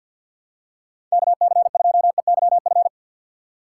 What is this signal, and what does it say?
Morse code, heard as a keyed tone, about 53 wpm, 700 Hz. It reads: KC2EYF
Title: USA Call Signs - 6 Characters - 4x Character Spacing - 50wpm